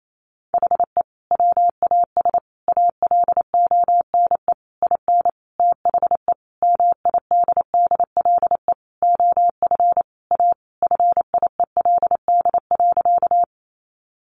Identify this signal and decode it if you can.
Morse code, heard as a keyed tone, about 28 wpm, 700 Hz. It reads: HE WAS ALONE IN THE MIDDLE OF A FIELD.
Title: Sentences from Top 1000 Words 28wpm